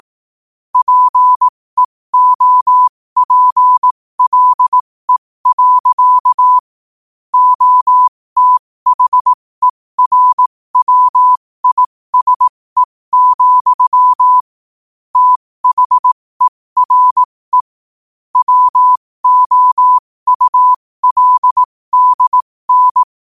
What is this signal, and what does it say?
Morse code, heard as a keyed tone, about 18 wpm, spaced as for 15 wpm, 1000 Hz. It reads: PEOPLE. OTHERWISE, THERE WOULDN
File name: qotdfiles / 15wpm